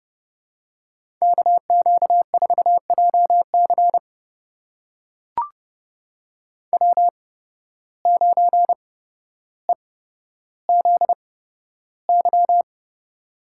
Morse code, heard as a keyed tone, about 30 wpm, spaced as for 13 wpm, 700 Hz. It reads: KQ4JC W9EZY
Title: USA Call Signs - 5 Characters - 8x Character Spacing - 30wpm